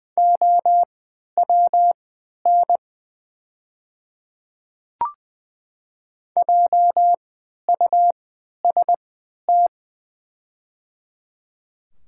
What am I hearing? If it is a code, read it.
Morse code, heard as a keyed tone, about 20 wpm, spaced as for 11 wpm, 700 Hz. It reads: OWN JUST
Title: 3x Character Spacing - Top 300 Words - 20wpm